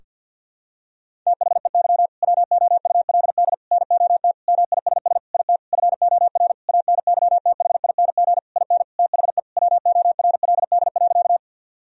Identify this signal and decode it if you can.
Morse code, heard as a keyed tone, about 50 wpm, 700 Hz. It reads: THEY WOULD NOT MISS IT FOR ANYTHING IN THE WORLD.
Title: Sentences from Top 1000 Words - Review Part 3 - 50wpm